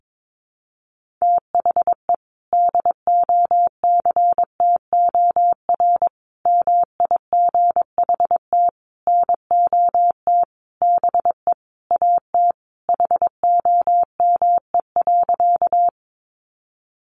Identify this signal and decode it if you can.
Morse code, heard as a keyed tone, about 22 wpm, 700 Hz. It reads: THE DOCTOR MIGHT NOT BE AT HOME.
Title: Sentences from Top 1500 Words 22wpm